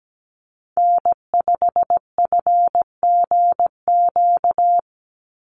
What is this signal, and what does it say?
Morse code, heard as a keyed tone, about 17 wpm, 700 Hz. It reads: N5FGQ
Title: USA Call Signs 17wpm